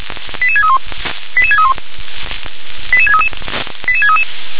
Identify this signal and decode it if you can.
Five-tone selective call (ZVEI), heard as quick runs of five tones, tones of about 70 ms. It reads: 88631 88631 88633 88633